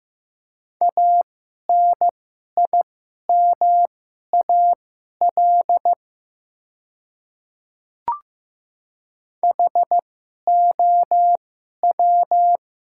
Morse code, heard as a keyed tone, about 15 wpm, spaced as for 11 wpm, 690 Hz. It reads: ANIMAL HOW
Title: 2x Character Spacing - Top 500 Words - 15wpm